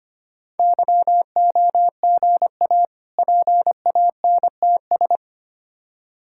Morse code, heard as a keyed tone, about 25 wpm, 700 Hz. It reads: YOGA PANTS
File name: practice / 25wpm / Difficult Words with Hints - 25wpm